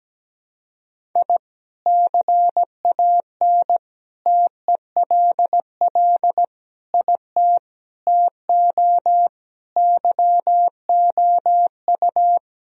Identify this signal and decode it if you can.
Morse code, heard as a keyed tone, about 17 wpm, 700 Hz. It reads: I CAN TELL IT TO YOU
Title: Sentences from Top 1000 Words - Review Part 1 - 17wpm